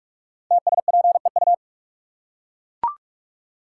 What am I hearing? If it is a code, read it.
Morse code, heard as a keyed tone, about 44 wpm, 700 Hz. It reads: TSPEU
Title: Code Groups - 5 Letter:Numbers - 45wpm